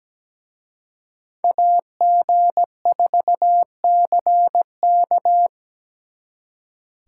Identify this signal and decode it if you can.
Morse code, heard as a keyed tone, about 20 wpm, 700 Hz. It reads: AG4CK
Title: Speed Racing - USA 2x2 Call Signs 17wpm